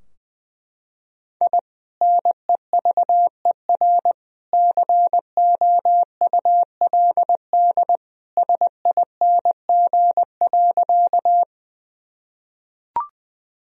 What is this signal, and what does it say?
Morse code, heard as a keyed tone, about 21 wpm, 700 Hz. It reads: I NEVER COULD SING.
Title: Sentences from Top 1000 Words - Review Part 2 - 20wpm